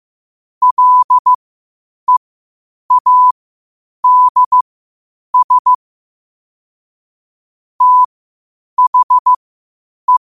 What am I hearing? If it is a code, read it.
Morse code, heard as a keyed tone, about 15 wpm, spaced as for 8 wpm, 1000 Hz. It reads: LEADS THE
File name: qotdfiles / 8wpm